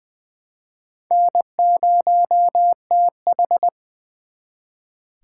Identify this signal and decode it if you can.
Morse code, heard as a keyed tone, about 24 wpm, 700 Hz. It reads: N0TH
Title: Speed Racing - USA 1x2 Call Signs 20wpm